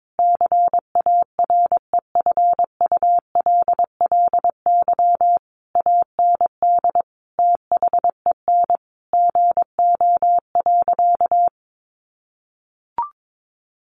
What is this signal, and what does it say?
Morse code, heard as a keyed tone, about 22 wpm, 700 Hz. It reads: CAREFULLY AND THEN GO.
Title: Sentences from Top 1300 Words 22wpm